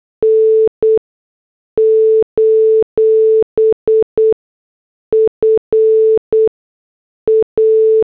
Morse code, heard as a keyed tone, about 8 wpm, 440 Hz. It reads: N : F A
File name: morse / headlines